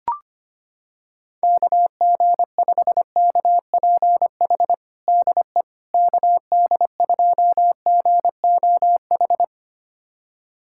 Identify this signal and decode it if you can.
Morse code, heard as a keyed tone, about 25 wpm, 700 Hz. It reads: KG5KPH DE KD2GOH
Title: USA Callsign DE Callsign 25wpm